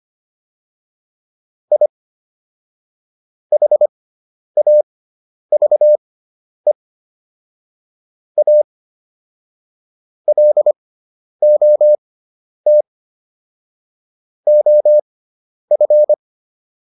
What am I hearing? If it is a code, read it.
Morse code, heard as a keyed tone, about 25 wpm, spaced as for 10 wpm, 600 Hz. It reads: I HAVE A LOT OF
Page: intermediate-practice-files